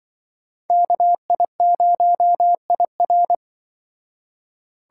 Morse code, heard as a keyed tone, about 24 wpm, 700 Hz. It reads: KI0IR
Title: Speed Racing - USA 2x2 Call Signs 20wpm